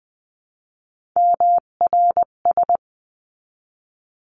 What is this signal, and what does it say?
Morse code, heard as a keyed tone, about 20 wpm, 700 Hz. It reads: MRS
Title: 258 Lesson 42c - Words 16-20wpm